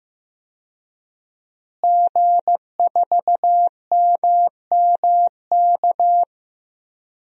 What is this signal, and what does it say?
Morse code, heard as a keyed tone, about 15 wpm, 700 Hz. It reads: G4MMK
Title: DXCC-100 Call Signs - 5 Characters - Repeated 3x - 15wpm